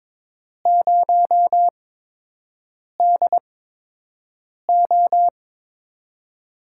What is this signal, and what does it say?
Morse code, heard as a keyed tone, about 22 wpm, spaced as for 6 wpm, 700 Hz. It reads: 0DO
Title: USA Call Signs - 8x Character Spacing - 22wpm